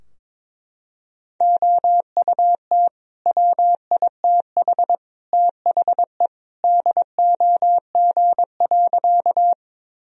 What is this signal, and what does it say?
Morse code, heard as a keyed tone, about 22 wpm, 700 Hz. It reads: OUT WITH THE DOG.
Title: Sentences from Top 1000 Words - Review Part 2 - 22wpm